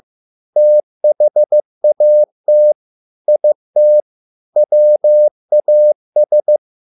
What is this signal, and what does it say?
Morse code, heard as a keyed tone, about 15 wpm, 600 Hz. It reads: THAT IT WAS